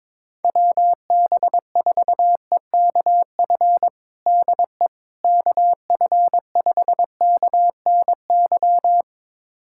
Morse code, heard as a keyed tone, about 22 wpm, 700 Hz. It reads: WB4EKF DE KF5KNY
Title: USA Callsign DE Callsign - 6 Characters - 22wpm